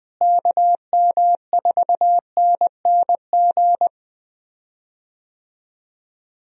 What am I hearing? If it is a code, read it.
Morse code, heard as a keyed tone, about 20 wpm, 700 Hz. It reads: KM4NNG
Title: USA Call Signs - 6 Characters - Repeated 2x - 20wpm